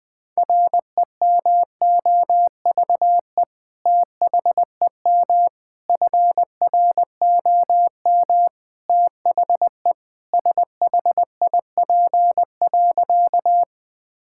Morse code, heard as a keyed tone, about 20 wpm, 700 Hz. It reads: REMOVE THEM FROM THE SHIP.
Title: Sentences from Top 1000 Words - Review Part 5 - 20wpm